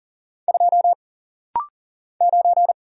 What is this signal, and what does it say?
Morse code, heard as a keyed tone, about 40 wpm, 700 Hz. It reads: 2 9
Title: Single Letter-Number Alt - Rapid-Fire 40wpm